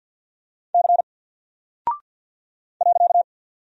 Morse code, heard as a keyed tone, about 48 wpm, 700 Hz. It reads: C .
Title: Single Character (Letter:Number:Punctuation:Prosign) 50wpm